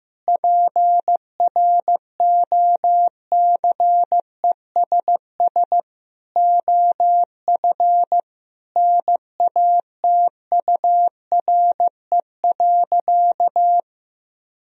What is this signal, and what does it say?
Morse code, heard as a keyed tone, about 15 wpm, 700 Hz. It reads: PROCESS OF NATURE.
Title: Sentences from Top 1000 Words - Review Part 3 - 15wpm